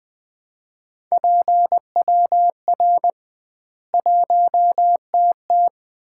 Morse code, heard as a keyed tone, about 20 wpm, 700 Hz. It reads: PWR 1TT
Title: Common QSO Phrases - 2x Word-Spacing - 20wpm